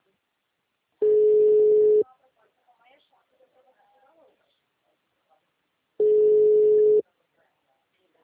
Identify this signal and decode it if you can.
Ringback tone (Europe, 425 Hz), with speech.